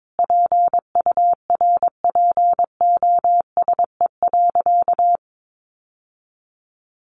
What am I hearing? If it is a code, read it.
Morse code, heard as a keyed tone, about 22 wpm, 700 Hz. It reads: PURPOSE.
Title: Sentences from Top 800 Words 22wpm